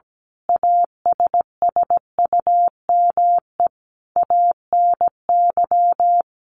Morse code, heard as a keyed tone, about 17 wpm, 700 Hz. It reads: ASSUME ANY
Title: Sentences from Top 1500 Words 17wpm